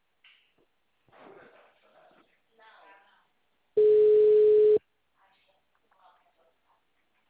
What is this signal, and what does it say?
Ringback tone (Europe, 425 Hz), with speech.